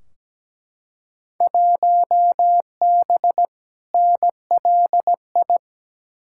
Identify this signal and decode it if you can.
Morse code, heard as a keyed tone, about 17 wpm, 700 Hz. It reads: 1B NLI
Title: Field Day Exchanges - 17wpm